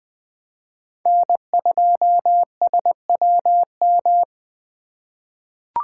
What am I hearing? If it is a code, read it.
Morse code, heard as a keyed tone, about 20 wpm, 700 Hz. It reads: N2SWM
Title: USA Call Signs - 5 Characters - 3x Character Spacing - 20wpm